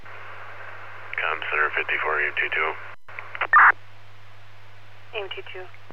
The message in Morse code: none

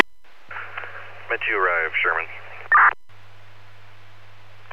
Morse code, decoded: none

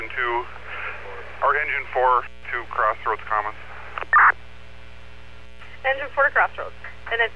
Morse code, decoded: none

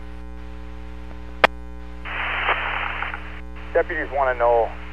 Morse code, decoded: none

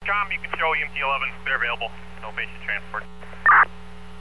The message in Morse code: none

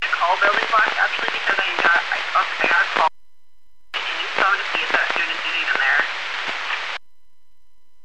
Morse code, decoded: none

